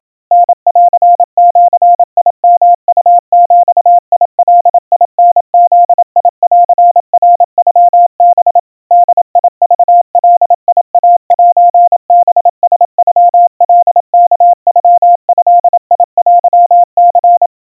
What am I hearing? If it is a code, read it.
Morse code (Japanese Wabun code), heard as a keyed tone, about 28 wpm, 700 Hz. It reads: タンジヨウビガタブンナノハ ボクガイセハラノカワノドテニ
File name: kawa / Morse sample / kamyu 100 cpm 700 Hz Crispy